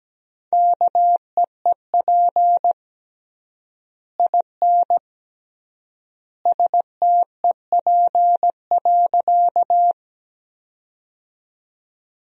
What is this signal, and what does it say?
Morse code, heard as a keyed tone, about 17 wpm, 700 Hz. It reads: KEEP IN STEP.
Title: 3x Word-Spacing - Sentences from Top 400 Words - 17wpm